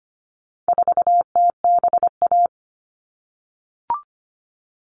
Morse code, heard as a keyed tone, about 25 wpm, 700 Hz. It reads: 4TBA
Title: All Cut Numbers - 4-Digit 25wpm